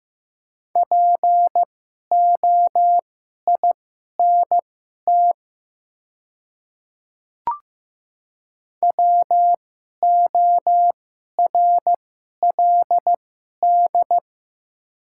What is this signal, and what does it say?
Morse code, heard as a keyed tone, about 15 wpm, spaced as for 11 wpm, 700 Hz. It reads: POINT WORLD
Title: 2x Character Spacing - Top 200 Words - 15wpm